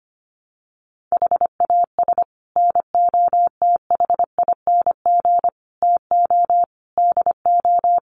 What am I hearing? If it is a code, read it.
Morse code, heard as a keyed tone, about 25 wpm, 700 Hz. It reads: HAS NOTHING TO DO
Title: Sets of 5 Words 25wpm